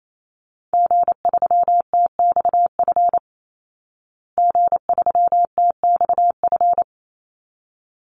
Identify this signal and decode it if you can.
Morse code, heard as a keyed tone, about 28 wpm, 700 Hz. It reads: G3TXF G3TXF
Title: Call Signs Repeated 3x - 28wpm